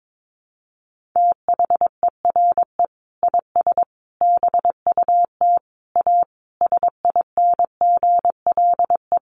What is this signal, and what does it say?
Morse code, heard as a keyed tone, about 22 wpm, 700 Hz. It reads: THERE IS BUT A SINGLE
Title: Sentences from Top 1400 Words 22wpm